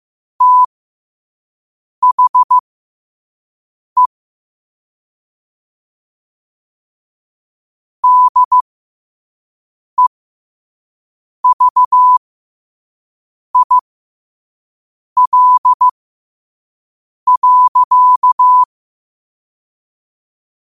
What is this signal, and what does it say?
Morse code, heard as a keyed tone, about 15 wpm, spaced as for 5 wpm, 1000 Hz. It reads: THE DEVIL.
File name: qotdfiles / 5wpm